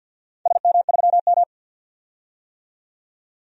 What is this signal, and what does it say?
Morse code, heard as a keyed tone, about 47 wpm, 700 Hz. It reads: SM3M
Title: DXCC-100 Call Signs - 4 Characters - 50wpm